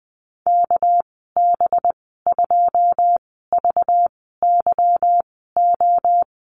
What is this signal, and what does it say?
Morse code, heard as a keyed tone, about 20 wpm, spaced as for 14 wpm, 700 Hz. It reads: KB2VYO
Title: 238 Lesson 38g - Letter q - Call Signs 20wpm